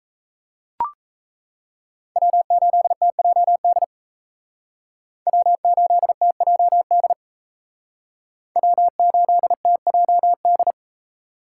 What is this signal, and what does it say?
Morse code, heard as a keyed tone, about 33 wpm, 700 Hz. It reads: W8TJD W8TJD W8TJD
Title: Speed Racing - USA 1x3 Call Signs 28wpm